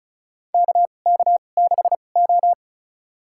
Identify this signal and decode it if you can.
Morse code, heard as a keyed tone, about 36 wpm, spaced as for 25 wpm, 700 Hz. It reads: KK6O
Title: USA Call Signs - 4 Characters - 2x Character Spacing - 35wpm